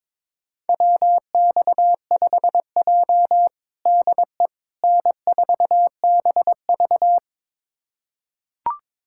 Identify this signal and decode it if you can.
Morse code, heard as a keyed tone, about 22 wpm, 700 Hz. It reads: WX5J DE N4BV
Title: USA Callsign DE Callsign - 4 Characters - 22wpm